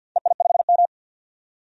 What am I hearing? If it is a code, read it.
Morse code, heard as a keyed tone, about 47 wpm, 700 Hz. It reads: EI5M
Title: DXCC-100 Call Signs - 4 Characters - 50wpm